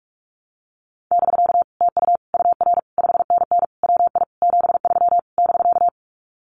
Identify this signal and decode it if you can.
Morse code, heard as a keyed tone, about 46 wpm, 700 Hz. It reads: <BK> TU UR 5NN WI 73 <BK>
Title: Common QSO Phrases Encore 45wpm